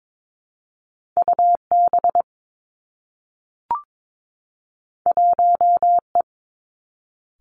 Morse code, heard as a keyed tone, about 22 wpm, 700 Hz. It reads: U B 1 E
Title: Two Challenging Letter-Number Combinations 22wpm